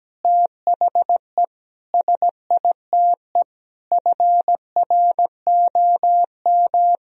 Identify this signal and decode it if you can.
Morse code, heard as a keyed tone, about 17 wpm, 700 Hz. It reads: THE SITE FROM
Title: Sentences from Top 1000 Words - Review Part 4 - 17wpm